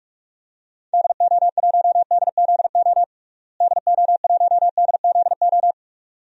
Morse code, heard as a keyed tone, about 46 wpm, 700 Hz. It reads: DO1DZO DO1DZO
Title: DXCC-100 Call Signs - 6 Characters - Repeated 2x - 45wpm